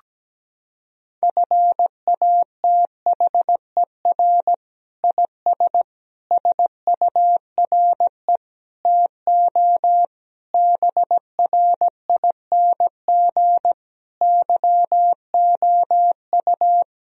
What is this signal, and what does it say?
Morse code, heard as a keyed tone, about 17 wpm, 700 Hz. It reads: FATHER IS SURE TO BRING YOU
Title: Sentences from Top 1000 Words - Review Part 3 - 17wpm